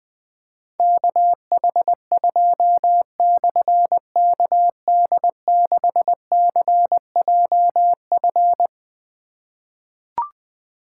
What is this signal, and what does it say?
Morse code, heard as a keyed tone, about 21 wpm, 700 Hz. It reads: KH2/KD6CJF
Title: Difficult Call Signs 20wpm